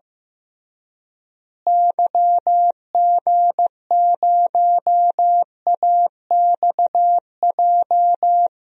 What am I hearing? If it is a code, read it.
Morse code, heard as a keyed tone, about 15 wpm, 700 Hz. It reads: YG0AXJ
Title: DXCC-100 Call Signs - 6 Characters - Repeated 3x - 15wpm